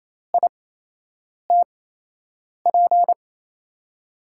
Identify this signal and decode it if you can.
Morse code, heard as a keyed tone, about 28 wpm, spaced as for 8 wpm, 700 Hz. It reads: ITP
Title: USA Call Signs - 6 Characters - 8x Character Spacing - 28wpm